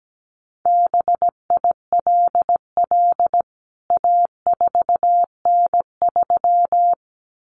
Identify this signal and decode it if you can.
Morse code, heard as a keyed tone, about 17 wpm, 700 Hz. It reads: BILL A4N3
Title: CWT Exchange 17wpm